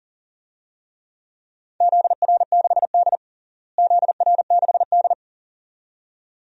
Morse code, heard as a keyed tone, about 40 wpm, 700 Hz. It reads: ZR6D ZR6D
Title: DXCC-100 Call Signs - 4 Characters - Repeated 2x - 40wpm